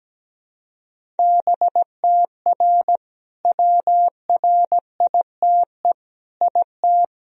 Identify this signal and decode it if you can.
Morse code, heard as a keyed tone, about 17 wpm, 700 Hz. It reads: BTR WRITE IT
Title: Sentences from Top 400 Words and Abbreviations 17wpm